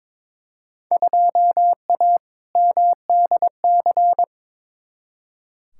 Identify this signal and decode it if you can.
Morse code, heard as a keyed tone, about 23 wpm, 700 Hz. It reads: 2A MDC
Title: Field Day Exchanges - 22wpm